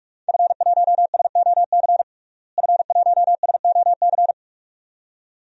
Morse code, heard as a keyed tone, about 46 wpm, 700 Hz. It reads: F1SOC F1SOC
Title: DXCC-100 Call Signs - 5 Characters - Repeated 2x - 45wpm